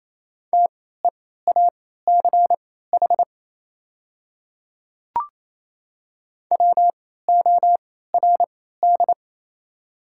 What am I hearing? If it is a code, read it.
Morse code, heard as a keyed tone, about 28 wpm, spaced as for 16 wpm, 690 Hz. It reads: TEACH WORD
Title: 3x Character Spacing - Top 500 Words - 28wpm